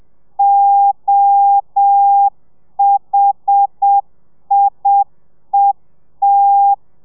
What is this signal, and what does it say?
Morse code, heard as a keyed tone, about 7 wpm, 800 Hz. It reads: OHIET
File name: telegrafia / i2tfj / Lez1 30